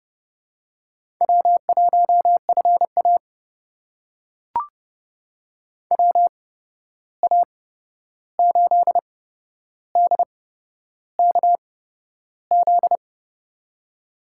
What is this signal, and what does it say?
Morse code, heard as a keyed tone, about 30 wpm, spaced as for 11 wpm, 700 Hz. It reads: W1FA WA8DKZ
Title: USA Call Signs - 8x Character Spacing - 30wpm